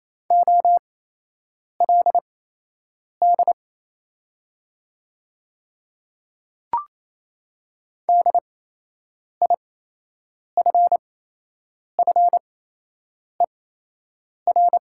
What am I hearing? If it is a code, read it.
Morse code, heard as a keyed tone, about 27 wpm, spaced as for 8 wpm, 700 Hz. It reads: OLD DIFFER
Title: 8x Character Spacing - Top 200 Words - 28wpm